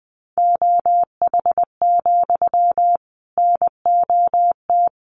Morse code, heard as a keyed tone, about 19 wpm, 700 Hz. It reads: OH, NOT
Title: Sentences from Top 1200 Words 20wpm